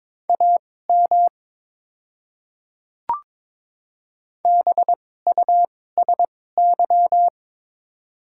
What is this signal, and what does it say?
Morse code, heard as a keyed tone, about 22 wpm, spaced as for 16 wpm, 700 Hz. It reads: AM BUSY
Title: 2x Character Spacing - Top 500 Words - 22wpm